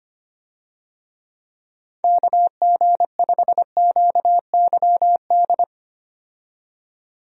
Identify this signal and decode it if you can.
Morse code, heard as a keyed tone, about 25 wpm, 700 Hz. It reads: KG5QYD
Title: USA Call Signs - 6 Characters - 25wpm